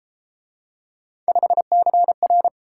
Morse code, heard as a keyed tone, about 31 wpm, 700 Hz. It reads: 5CR